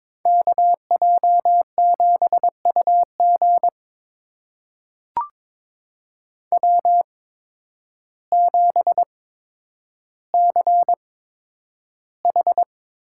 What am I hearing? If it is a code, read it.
Morse code, heard as a keyed tone, about 22 wpm, spaced as for 10 wpm, 700 Hz. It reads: KJ7UG W7CH
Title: USA Call Signs - 5 Characters - 8x Character Spacing - 22wpm